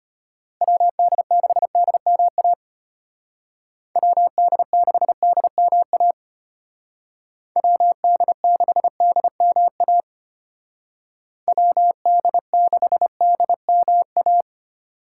Morse code, heard as a keyed tone, about 30 wpm, 700 Hz. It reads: WD6DMA WD6DMA WD6DMA WD6DMA
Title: Speed Racing - USA 2x3 Call Signs 25wpm